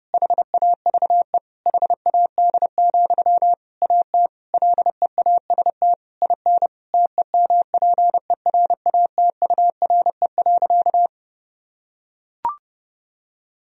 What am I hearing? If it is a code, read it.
Morse code, heard as a keyed tone, about 32 wpm, 700 Hz. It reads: HAVE HAD, AT LEAST IN TEMPERATURE.
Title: Sentences from Top 1000 Words - Review Part 4 - 30wpm